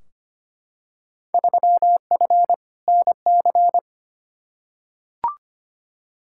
Morse code, heard as a keyed tone, about 25 wpm, 700 Hz. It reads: 3F NC
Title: Field Day Exchanges - 25wpm